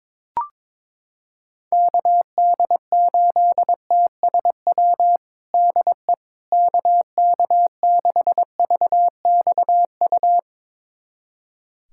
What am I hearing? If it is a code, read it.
Morse code, heard as a keyed tone, about 22 wpm, 700 Hz. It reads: KD8TSW DE KK6VXU
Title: USA Callsign DE Callsign - 6 Characters - 22wpm